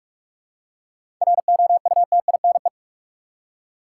Morse code, heard as a keyed tone, about 47 wpm, 700 Hz. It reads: ROUTINE
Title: Seven Letter Words - 45wpm